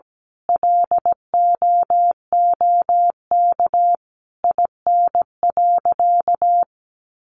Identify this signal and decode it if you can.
Morse code, heard as a keyed tone, about 17 wpm, 700 Hz. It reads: LOOK IN.
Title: Sentences from Top 1300 Words 17wpm